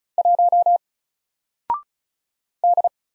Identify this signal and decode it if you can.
Morse code, heard as a keyed tone, about 36 wpm, 700 Hz. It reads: 1 D
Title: Single Letter-Number Alt 35wpm